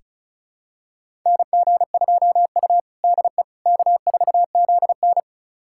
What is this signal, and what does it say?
Morse code, heard as a keyed tone, about 36 wpm, 700 Hz. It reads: NG2U DE K4ZN
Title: USA Callsign DE Callsign - 4 Characters - 35wpm